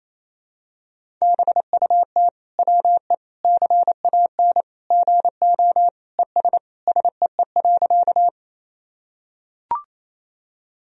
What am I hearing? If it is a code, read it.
Morse code, heard as a keyed tone, about 28 wpm, 700 Hz. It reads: BUT WE CAN GO ES SEE.
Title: Sentences from Top 100 Words and Abbreviations 28wpm